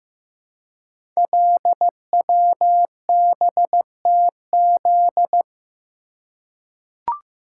Morse code, heard as a keyed tone, about 15 wpm, 700 Hz. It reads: LWBTZ
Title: Code Groups - 5 Letter:Numbers - 15wpm